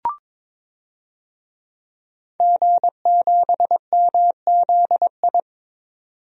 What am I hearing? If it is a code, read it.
Morse code, heard as a keyed tone, about 21 wpm, 700 Hz. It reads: G7MZI
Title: DXCC-100 Call Signs - 5 Characters - 22wpm